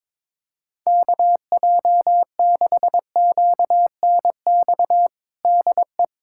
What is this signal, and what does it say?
Morse code, heard as a keyed tone, about 22 wpm, 700 Hz. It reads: KJ6QNX DE
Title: USA Callsign DE Callsign - 6 Characters - 22wpm